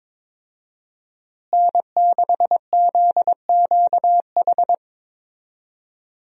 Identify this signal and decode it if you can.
Morse code, heard as a keyed tone, about 22 wpm, 700 Hz. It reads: N6ZQH
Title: USA Call Signs - 5 Characters - Repeated 2x - 22wpm